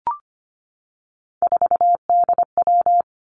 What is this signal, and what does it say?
Morse code, heard as a keyed tone, about 25 wpm, 700 Hz. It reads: 4DW
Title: Three Challenging Letter-Number Combinations 25wpm